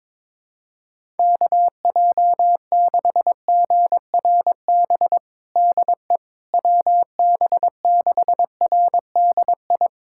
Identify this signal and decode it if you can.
Morse code, heard as a keyed tone, about 22 wpm, 700 Hz. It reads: KJ6GRB DE WB6RDI
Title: USA Callsign DE Callsign - 6 Characters - 22wpm